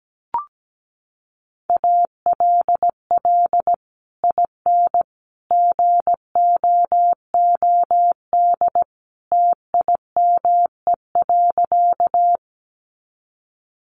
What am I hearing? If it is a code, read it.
Morse code, heard as a keyed tone, about 17 wpm, 700 Hz. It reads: ALL IN GOOD TIME.
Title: Sentences from Top 200 Words 17wpm